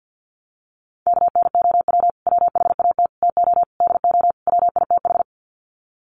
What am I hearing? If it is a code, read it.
Morse code, heard as a keyed tone, about 51 wpm, 700 Hz. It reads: KNOW WHAT TO DO WITH
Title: Sets of 5 Words 50wpm